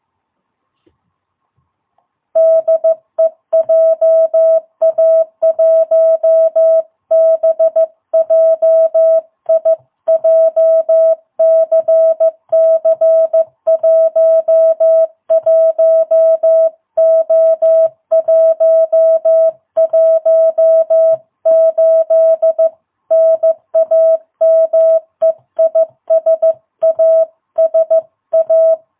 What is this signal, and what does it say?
Morse code, heard as a keyed tone, about 15 wpm, 660 Hz. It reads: DE JA1BJI JCC11O118 NAME IS ASA
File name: jh8cbh / 2022cw